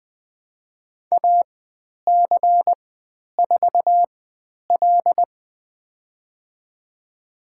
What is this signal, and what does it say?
Morse code, heard as a keyed tone, about 20 wpm, spaced as for 10 wpm, 700 Hz. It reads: AC4L
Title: 095 Lesson 16g - Letter c - Call Signs 20wpm